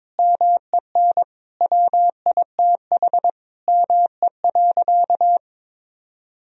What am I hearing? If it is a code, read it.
Morse code, heard as a keyed tone, about 22 wpm, 700 Hz. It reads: MEN WITH ME.
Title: Sentences from Top 1000 Words - Review Part 2 - 22wpm